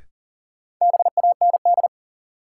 Morse code, heard as a keyed tone, about 40 wpm, 700 Hz. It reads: BAND
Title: Sentences from Top 1600 Words - New Word Drill Alt - 40wpm